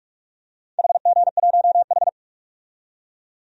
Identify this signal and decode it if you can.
Morse code, heard as a keyed tone, about 47 wpm, 700 Hz. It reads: HG1H
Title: DXCC-100 Call Signs - 4 Characters - 45wpm